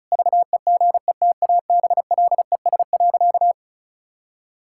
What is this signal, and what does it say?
Morse code, heard as a keyed tone, about 34 wpm, 700 Hz. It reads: VEGETABLES.
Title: Pangram Challenge v2 - 35wpm